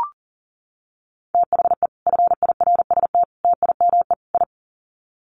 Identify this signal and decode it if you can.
Morse code, heard as a keyed tone, about 40 wpm, 700 Hz. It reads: THE FIRST TIME I
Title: Sets of 4 Words 40wpm